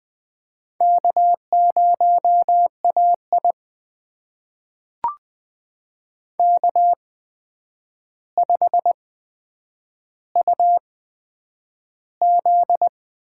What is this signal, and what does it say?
Morse code, heard as a keyed tone, about 20 wpm, spaced as for 9 wpm, 700 Hz. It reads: K0AI K5UZ
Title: USA Call Signs - 4 Characters - 8x Character Spacing - 20wpm